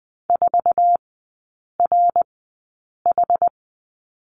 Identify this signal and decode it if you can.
Morse code, heard as a keyed tone, about 20 wpm, spaced as for 8 wpm, 700 Hz. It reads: 4RH
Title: 060 Lesson 11e - Letter h - Three Characters 20wpm